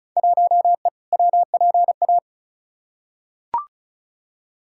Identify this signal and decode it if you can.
Morse code, heard as a keyed tone, about 34 wpm, 700 Hz. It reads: 1E WPA
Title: Field Day Exchanges - 35wpm